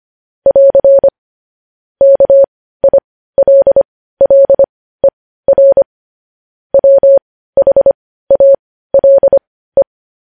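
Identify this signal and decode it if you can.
Morse code, heard as a keyed tone, about 25 wpm, spaced as for 15 wpm, 550 Hz. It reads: <AR> KILLER WHALE
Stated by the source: Morse practice sending news headlines at 15 wpm.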